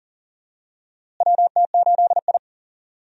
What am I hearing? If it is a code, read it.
Morse code, heard as a keyed tone, about 40 wpm, 700 Hz. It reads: WT8I
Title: USA Call Signs - 4 Characters - Repeated 2x - 40wpm